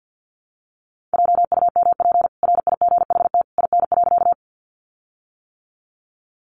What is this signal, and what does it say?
Morse code, heard as a keyed tone, about 50 wpm, 700 Hz. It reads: JUMP RIGHT IN.
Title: Sentences from Top 1300 Words 50wpm